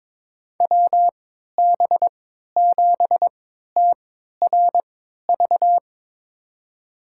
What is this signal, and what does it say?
Morse code, heard as a keyed tone, about 22 wpm, spaced as for 12 wpm, 700 Hz. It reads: WB7TRV
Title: USA Call Signs - 6 Characters - 3x Character Spacing - 22wpm